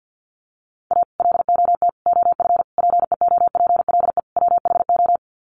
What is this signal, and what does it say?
Morse code, heard as a keyed tone, about 50 wpm, 700 Hz. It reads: A LOT OF PEOPLE WHO
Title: Sets of 5 Words 50wpm